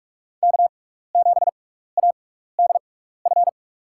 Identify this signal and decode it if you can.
Morse code, heard as a keyed tone, about 42 wpm, spaced as for 15 wpm, 700 Hz. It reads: K7ADF
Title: USA Call Signs - 6x Character Spacing - 45wpm